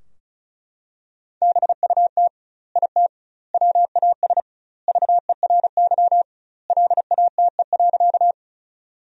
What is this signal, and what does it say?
Morse code, heard as a keyed tone, about 36 wpm, 700 Hz. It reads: BUT IT WAS VERY LATE.
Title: 2x Word-Spacing - Sentences from Top 300 Words - 35wpm